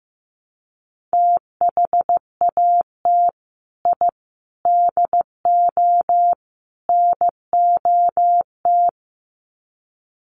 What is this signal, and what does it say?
Morse code, heard as a keyed tone, about 15 wpm, 700 Hz. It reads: THAT I DO NOT
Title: Sets of 4 Words 15wpm